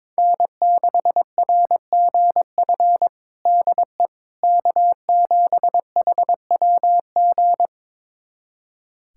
Morse code, heard as a keyed tone, about 22 wpm, 700 Hz. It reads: N6RGF DE K7HWG
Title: USA Callsign DE Callsign - 5 Characters - 22wpm